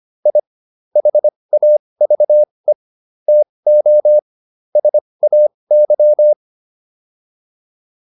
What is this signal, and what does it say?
Morse code, heard as a keyed tone, about 25 wpm, spaced as for 20 wpm, 600 Hz. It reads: I HAVE TO SAY